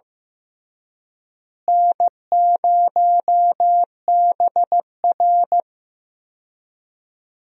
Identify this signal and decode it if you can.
Morse code, heard as a keyed tone, about 16 wpm, 700 Hz. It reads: N0BR
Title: USA Call Signs - 4 Characters - Repeated 3x - 15wpm